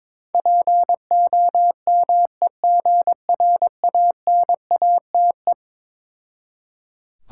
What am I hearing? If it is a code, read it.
Morse code, heard as a keyed tone, about 21 wpm, 690 Hz. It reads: POMEGRANATE
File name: Difficult Words with Hints - 22wpm